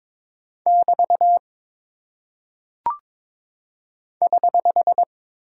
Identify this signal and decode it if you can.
Morse code, heard as a keyed tone, about 22 wpm, 700 Hz. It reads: <BT> <HH>
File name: practice / 22wpm / Single Punctuation-Prosign - 22wpm